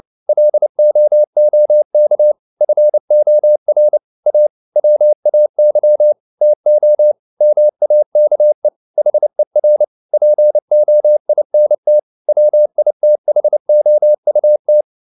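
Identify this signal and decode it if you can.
Morse code, heard as a keyed tone, about 29 wpm, 600 Hz. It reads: LOOK FOR A WAY TO MAKE HER POINT WITHOUT
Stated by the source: Morse code at 29 wpm, sending Ken Follett's novel Never in English, from CW med Gnister.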